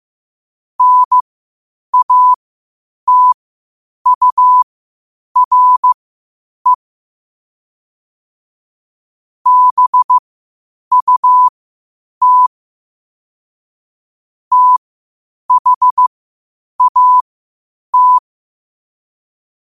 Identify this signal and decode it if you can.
Morse code, heard as a keyed tone, about 15 wpm, spaced as for 8 wpm, 1000 Hz. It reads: NATURE BUT THAT
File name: qotdfiles / 8wpm